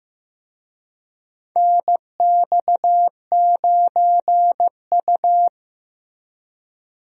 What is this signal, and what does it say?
Morse code, heard as a keyed tone, about 15 wpm, 700 Hz. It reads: NX9U
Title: USA Call Signs - 4 Characters - Repeated 3x - 15wpm